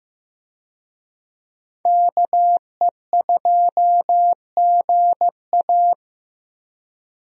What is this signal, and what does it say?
Morse code, heard as a keyed tone, about 15 wpm, 700 Hz. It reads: KE2GA
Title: USA Call Signs - 5 Characters - Repeated 3x - 15wpm